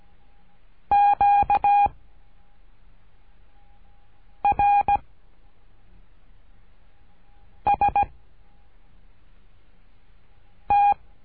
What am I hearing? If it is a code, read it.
Morse code, heard as a keyed tone, about 17 wpm, 810 Hz. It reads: Q R S T
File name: cursos / CURSO DE TELEGRAFIA / Telegrafia - Trilha 13